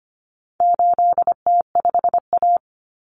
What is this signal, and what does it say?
Morse code, heard as a keyed tone, about 25 wpm, 700 Hz. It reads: 8T5A
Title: Common Cut Numbers - 4-Digit 25wpm